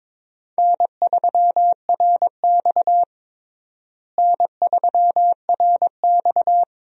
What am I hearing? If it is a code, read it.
Morse code, heard as a keyed tone, about 22 wpm, 700 Hz. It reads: N3RX N3RX
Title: USA Call Signs - 4 Characters - Repeated 3x - 22wpm